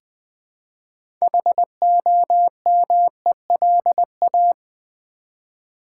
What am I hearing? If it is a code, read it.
Morse code, heard as a keyed tone, about 20 wpm, 700 Hz. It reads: HOMELA
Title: Build A Word - 2 Letters At A Time - 8-Letter Words - 20wpm